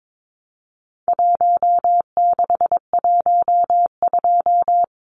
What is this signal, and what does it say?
Morse code, heard as a keyed tone, about 22 wpm, 700 Hz. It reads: 1612
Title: Numbers - Contest Style - 4-Digit 22wpm